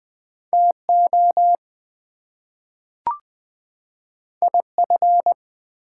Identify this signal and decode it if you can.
Morse code, heard as a keyed tone, about 20 wpm, 700 Hz. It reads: TO IF
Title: Two Letter Words - 20wpm